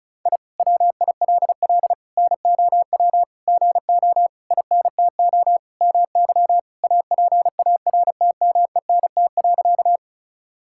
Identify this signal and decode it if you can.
Morse code, heard as a keyed tone, about 34 wpm, 700 Hz. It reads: I WILL NOW GO INTO MY APARTMENT.
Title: Sentences from Top 1000 Words - Review Part 6 - 35wpm